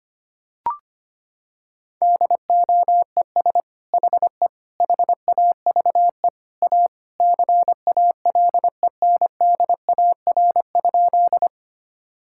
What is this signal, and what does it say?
Morse code, heard as a keyed tone, about 25 wpm, 700 Hz. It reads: DOES HE HAVE A CALENDAR?
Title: Sentences from Top 1600 Words 25wpm